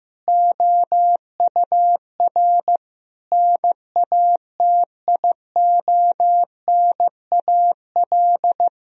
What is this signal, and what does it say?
Morse code, heard as a keyed tone, about 15 wpm, 700 Hz. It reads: OUR NATIONAL
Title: Sentences from Top 1000 Words - Review Part 4 - 15wpm